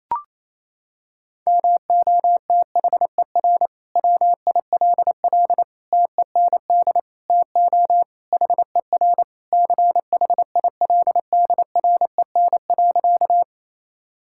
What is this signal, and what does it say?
Morse code, heard as a keyed tone, about 28 wpm, 700 Hz. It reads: MOTHER WILL TEND TO HER CHILDREN.
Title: Sentences from Top 1000 Words - Review Part 5 - 28wpm